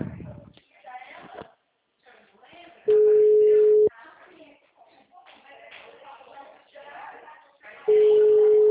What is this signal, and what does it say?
Ringback tone (Europe, 425 Hz), with speech.